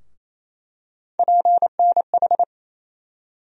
Morse code, heard as a keyed tone, about 28 wpm, 700 Hz. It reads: PNH
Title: USA Call Sign Postfixes - 3 Letters 28wpm